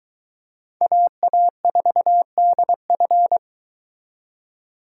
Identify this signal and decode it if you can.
Morse code, heard as a keyed tone, about 22 wpm, 700 Hz. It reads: AA4DF